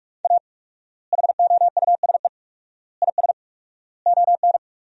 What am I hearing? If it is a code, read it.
Morse code, heard as a keyed tone, about 41 wpm, 700 Hz. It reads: A HOUSE IS ON
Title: 4x Word-Spacing - Sentences from Top 400 Words - 45wpm